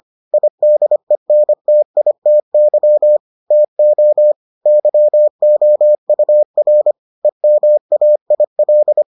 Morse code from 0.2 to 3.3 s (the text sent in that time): IDENTITY